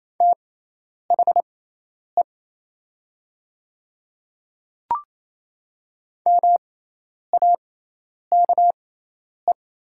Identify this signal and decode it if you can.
Morse code, heard as a keyed tone, about 28 wpm, spaced as for 10 wpm, 700 Hz. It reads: THE MAKE
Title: 6x Character Spacing - Top 200 Words - 28wpm